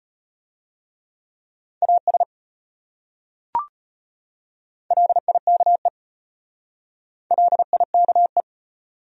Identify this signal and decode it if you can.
Morse code, heard as a keyed tone, about 33 wpm, 700 Hz. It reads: AS LIKE LIKE